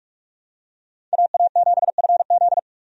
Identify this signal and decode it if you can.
Morse code, heard as a keyed tone, about 42 wpm, 700 Hz. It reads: AA7FZ
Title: USA Call Signs - 5 Characters - Repeated 2x - 45wpm